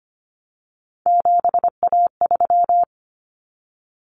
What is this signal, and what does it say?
Morse code, heard as a keyed tone, about 25 wpm, 700 Hz. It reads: 7A3
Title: Common Cut Numbers - 3-Digit 25wpm